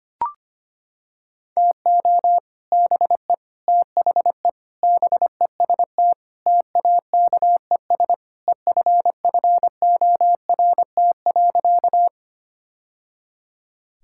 Morse code, heard as a keyed tone, about 25 wpm, 700 Hz. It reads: TO BE THE BEST TAKES EFFORT.
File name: practice / 25wpm / Sentences with 3-Word Phrases - 25wpm